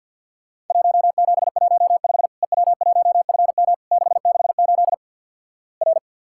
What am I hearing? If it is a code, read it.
Morse code, heard as a keyed tone, about 50 wpm, 700 Hz. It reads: 1715 EW1FM 668 R
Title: Worked All Europe DX Contest - QTC Exchange with Repeats - 50wpm